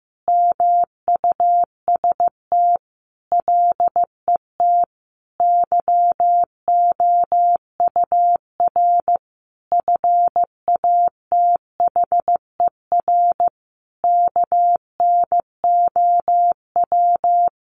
Morse code, heard as a keyed tone, about 15 wpm, 700 Hz. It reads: MUST LET YOUR FATHER KNOW
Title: Sentences from Top 1500 Words 15wpm